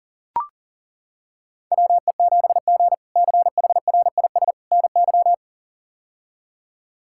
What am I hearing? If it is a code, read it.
Morse code, heard as a keyed tone, about 40 wpm, 700 Hz. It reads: WE7G CHRIS NY
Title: K1USN SST QSO Practice - 40wpm